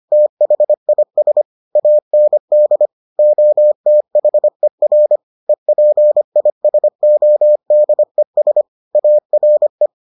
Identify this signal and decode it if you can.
Morse code, heard as a keyed tone, about 25 wpm, 600 Hz. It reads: THIS AND OTHER EPISODES ARE